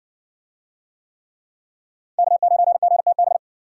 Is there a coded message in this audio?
Yes: Morse code from 2.2 to 3.4 s, heard as a keyed tone, about 58 wpm, 700 Hz. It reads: K9GTB